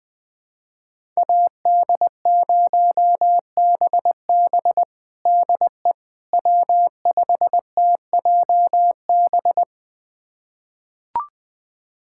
Morse code, heard as a keyed tone, about 20 wpm, 700 Hz. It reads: AD0BB DE W5TJB
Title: USA Callsign DE Callsign - 5 Characters - 20wpm